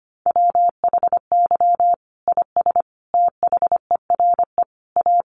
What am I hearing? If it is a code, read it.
Morse code, heard as a keyed tone, about 25 wpm, 700 Hz. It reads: WHY IS THERE A
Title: Sentences from Top 1300 Words 25wpm